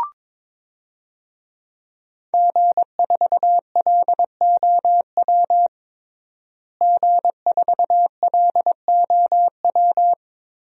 Morse code, heard as a keyed tone, about 22 wpm, 700 Hz. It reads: G4LOW G4LOW
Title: DXCC-100 Call Signs - 5 Characters - Repeated 2x - 22wpm